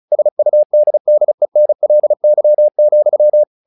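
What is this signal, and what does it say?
Morse code, heard as a keyed tone, about 34 wpm, 600 Hz. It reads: SUDDENLY,
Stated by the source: Morse practice text "Tower Trouble" sent at 35 WPM, from the SKCC - Straight Key Century Club learning center audio files.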